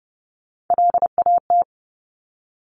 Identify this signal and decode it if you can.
Morse code, heard as a keyed tone, about 30 wpm, 690 Hz. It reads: LAT
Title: Top Three Letter Combinations in Words 30wpm